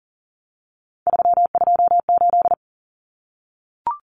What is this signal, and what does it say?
Morse code, heard as a keyed tone, about 40 wpm, 700 Hz. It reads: 328
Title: Common Cut Numbers - 3-Digit 40wpm